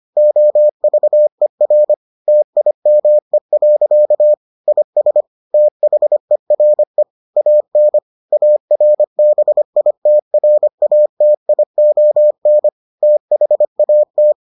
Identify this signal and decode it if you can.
Morse code, heard as a keyed tone, about 25 wpm, 600 Hz. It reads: OVER TIME. IS THERE AN ARBITRATION THAT